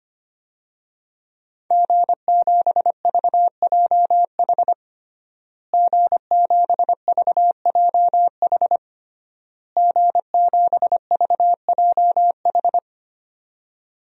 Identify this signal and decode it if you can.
Morse code, heard as a keyed tone, about 25 wpm, 700 Hz. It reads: G7VJH G7VJH G7VJH
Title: DXCC-100 Call Signs - 5 Characters - Repeated 3x - 25wpm